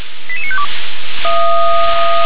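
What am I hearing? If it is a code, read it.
Five-tone selective call (ZVEI), heard as quick runs of five tones, tones of about 70 ms. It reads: 88952
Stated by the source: Five-tone ZVEI selective call received on an SDR.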